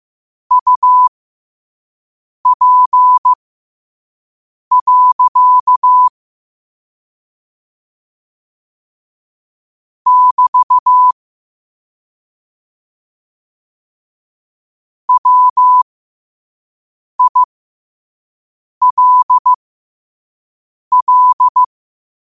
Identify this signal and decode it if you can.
Morse code, heard as a keyed tone, about 15 wpm, spaced as for 5 wpm, 1000 Hz. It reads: UP. <BT> WILL
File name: qotdfiles / 5wpm